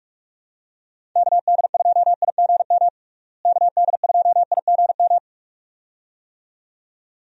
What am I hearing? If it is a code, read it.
Morse code, heard as a keyed tone, about 41 wpm, 700 Hz. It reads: KD2IGM KD2IGM
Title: USA Call Signs - 6 Characters - Repeated 2x - 45wpm